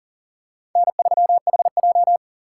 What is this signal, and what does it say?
Morse code, heard as a keyed tone, about 39 wpm, 700 Hz. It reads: N3HJ
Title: USA Call Signs - 4 Characters - 4x Character Spacing - 40wpm